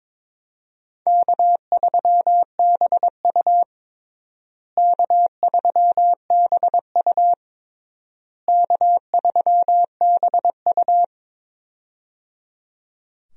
Morse code, heard as a keyed tone, about 21 wpm, 700 Hz. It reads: K3BU K3BU K3BU
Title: USA Call Signs - 4 Characters - Repeated 3x - 22wpm